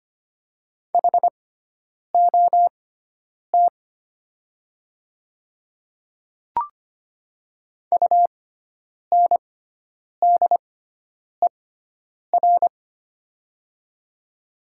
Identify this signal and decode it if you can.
Morse code, heard as a keyed tone, about 25 wpm, spaced as for 9 wpm, 700 Hz. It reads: HOT UNDER
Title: 6x Character Spacing - Top 200 Words - 25wpm